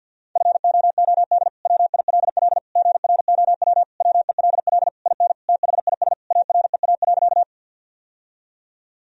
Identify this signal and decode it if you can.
Morse code, heard as a keyed tone, about 51 wpm, 700 Hz. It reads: FOOD WILL GROW WELL IN THIS AREA.
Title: Sentences from Top 1000 Words - Review Part 2 - 50wpm